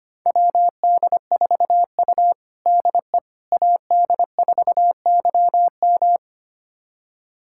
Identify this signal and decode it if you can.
Morse code, heard as a keyed tone, about 25 wpm, 700 Hz. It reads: WD4U DE AD4YM
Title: USA Callsign DE Callsign 25wpm